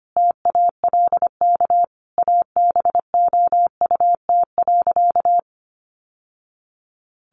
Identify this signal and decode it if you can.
Morse code, heard as a keyed tone, about 25 wpm, 700 Hz. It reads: TALK ABOUT.
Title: Sentences from Top 800 Words 25wpm